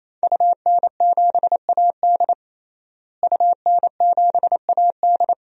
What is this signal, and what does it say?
Morse code, heard as a keyed tone, about 28 wpm, 700 Hz. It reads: UN7AD UN7AD
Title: DXCC-100 Call Signs - 5 Characters - Repeated 3x - 28wpm